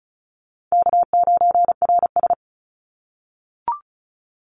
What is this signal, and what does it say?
Morse code, heard as a keyed tone, about 34 wpm, 700 Hz. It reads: K9RS
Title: Call Signs - 35wpm